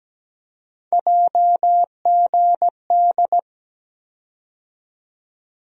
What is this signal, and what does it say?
Morse code, heard as a keyed tone, about 17 wpm, 700 Hz. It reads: JGD
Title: USA Call Sign Postfixes - 3 Letters 17wpm